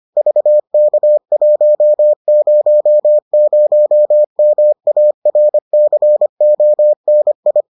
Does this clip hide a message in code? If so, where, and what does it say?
Yes: Morse code from 0.2 to 7.6 s, heard as a keyed tone, about 25 wpm, 600 Hz. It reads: VK100MARCONI